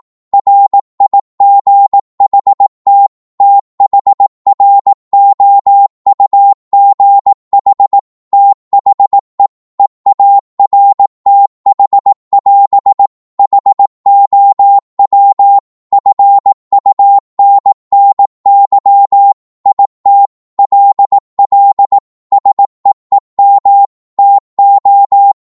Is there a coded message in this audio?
Yes: Morse code from 0.3 to 25.3 s, heard as a keyed tone, about 18 wpm, 800 Hz. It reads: RIGHT THROUGH THE EARTH<AS> HOW FUNNY IT LL SEEM TO